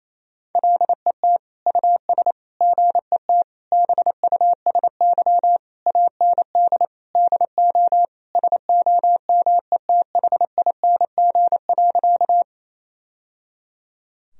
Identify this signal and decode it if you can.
Morse code, heard as a keyed tone, about 28 wpm, 690 Hz. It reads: LET US GET BUSY AND DO SOMETHING.
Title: Sentences from Top 1000 Words - Review Part 3 - 28wpm